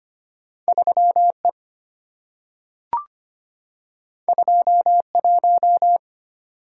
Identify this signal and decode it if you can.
Morse code, heard as a keyed tone, about 25 wpm, 700 Hz. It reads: 3E 21
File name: Two Letter-Number Combinations - 25wpm